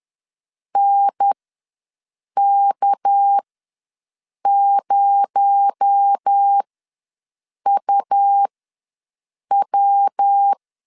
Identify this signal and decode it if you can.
Morse code, heard as a keyed tone, about 10 wpm, spaced as for 6 wpm, 800 Hz. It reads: NK0UW
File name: LEZIONI / 7 GF9Z / 7 G ws 10wpm 3843